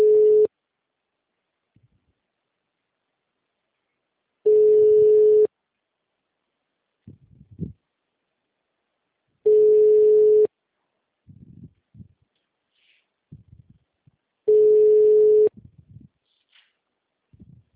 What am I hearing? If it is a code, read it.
Ringback tone (Europe, 425 Hz).
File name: out- -2011- - - .271